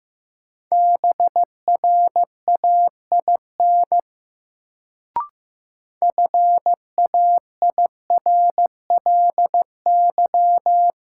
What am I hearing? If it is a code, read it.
Morse code, heard as a keyed tone, about 15 wpm, 700 Hz. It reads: BRAIN FAIRLY
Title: Sentences from Top 1100 Words - New Word Drill Alt - 15wpm